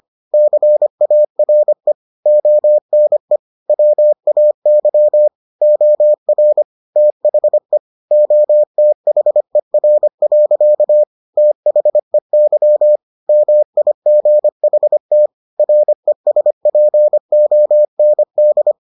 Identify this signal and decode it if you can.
Morse code, heard as a keyed tone, about 25 wpm, 600 Hz. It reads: CARE ONE WAY OR THE OTHER. THEY MIGHT RESPOND